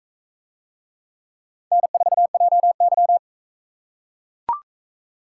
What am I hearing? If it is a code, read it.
Morse code, heard as a keyed tone, about 42 wpm, 700 Hz. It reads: N4JY